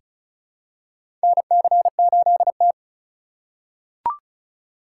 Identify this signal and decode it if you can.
Morse code, heard as a keyed tone, about 34 wpm, 700 Hz. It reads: NC8T
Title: USA Call Signs - 4 Characters - 3x Character Spacing - 35wpm